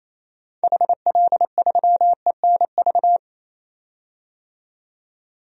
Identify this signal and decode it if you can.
Morse code, heard as a keyed tone, about 28 wpm, 700 Hz. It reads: HL3ENV
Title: DXCC-100 Call Signs - 6 Characters - 28wpm